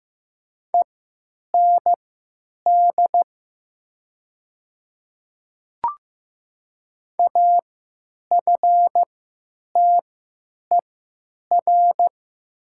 Morse code, heard as a keyed tone, about 15 wpm, spaced as for 9 wpm, 700 Hz. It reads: END AFTER
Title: 3x Character Spacing - Top 300 Words - 15wpm